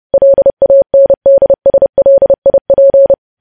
Morse code, heard as a keyed tone, about 30 wpm, 550 Hz. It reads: LANDSLIP
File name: podcast / News headlines-1x-30x30